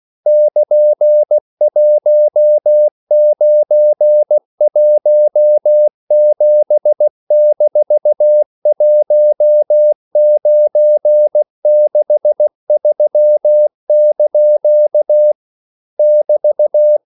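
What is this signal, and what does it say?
Morse code, heard as a keyed tone, about 16 wpm, 600 Hz. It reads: <KN>1917-1963) <BT>
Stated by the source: English quotations sent in CW as practice at 16 words per minute.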